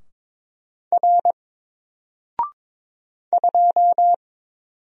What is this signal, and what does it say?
Morse code, heard as a keyed tone, about 22 wpm, 700 Hz. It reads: R 2
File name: Single-Letter-Number-22wpm